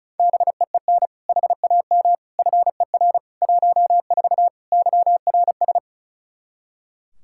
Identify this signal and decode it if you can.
Morse code, heard as a keyed tone, about 36 wpm, 700 Hz. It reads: BEEN HAM FER 14 YRS
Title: Practice QSO - Exchange 3 - More Info - 35wpm